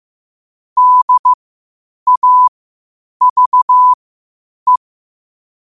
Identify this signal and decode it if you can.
Morse code, heard as a keyed tone, about 15 wpm, spaced as for 8 wpm, 1000 Hz. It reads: DAVE
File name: qotdfiles / 8wpm